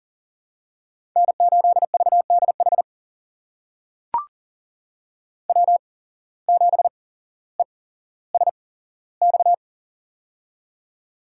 Morse code, heard as a keyed tone, about 39 wpm, spaced as for 17 wpm, 700 Hz. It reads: N8VDH W7ESX
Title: USA Call Signs - 5 Characters - 8x Character Spacing - 40wpm